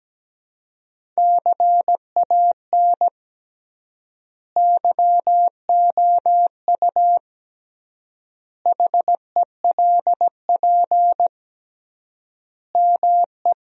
Morse code, heard as a keyed tone, about 17 wpm, 700 Hz. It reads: CAN YOU HELP ME
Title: 3x Word-Spacing - Sentences from Top 200 Words - 17wpm